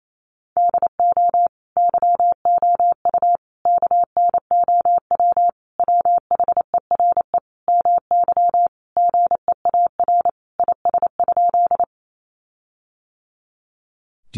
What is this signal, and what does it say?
Morse code, heard as a keyed tone, about 28 wpm, 700 Hz. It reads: DO YOU KNOW WHERE MY GEAR IS?
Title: Sentences from Top 1400 Words 28wpm